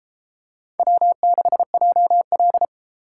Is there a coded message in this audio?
Yes: Morse code from 0.8 to 2.6 s, heard as a keyed tone, about 32 wpm, 700 Hz. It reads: W6JL